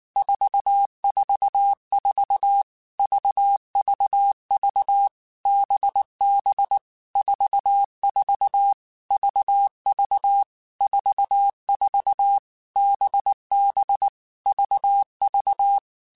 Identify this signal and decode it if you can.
Morse code, heard as a keyed tone, about 19 wpm, 800 Hz. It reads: 444 VVV BB 44 VV 44 BB VV
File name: Basic Letter Groups By7Groups / VB46 / VB46@18